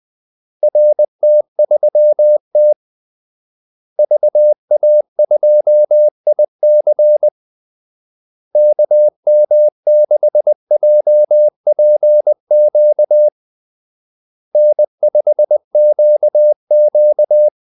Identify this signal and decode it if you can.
Morse code, heard as a keyed tone, about 20 wpm, 600 Hz. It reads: RT3T VA2IC KM6JPQ N5QQ
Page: Kaldesignaler 20 wpm